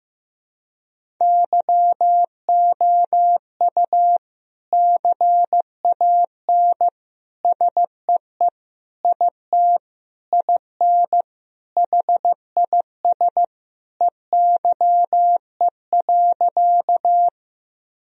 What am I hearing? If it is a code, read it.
Morse code, heard as a keyed tone, about 15 wpm, 700 Hz. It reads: YOU CAN SEE IT IN HIS EYE.
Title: Sentences from Top 1000 Words - Review Part 1 - 15wpm